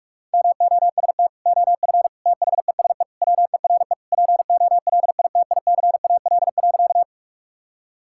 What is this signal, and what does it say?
Morse code, heard as a keyed tone, about 46 wpm, 700 Hz. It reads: MOST OF THESE WERE POLITICAL.
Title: Sentences from Top 1000 Words - Review Part 5 - 45wpm